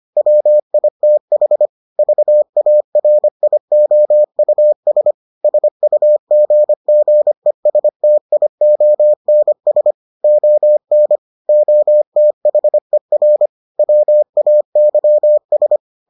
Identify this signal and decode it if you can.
Morse code, heard as a keyed tone, about 25 wpm, 600 Hz. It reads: WITH VARIOUS SUGGESTIONS ON OTHER WAYS